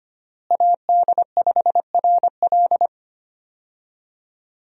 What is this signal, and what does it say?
Morse code, heard as a keyed tone, about 25 wpm, 700 Hz. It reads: AD5RL
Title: USA Call Signs - 5 Characters - Repeated 3x - 25wpm